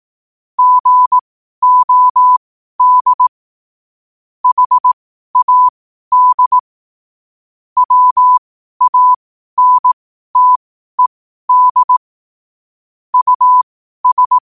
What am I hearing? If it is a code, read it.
Morse code, heard as a keyed tone, about 18 wpm, spaced as for 12 wpm, 1000 Hz. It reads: GOD HAD WANTED US
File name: qotdfiles / 12wpm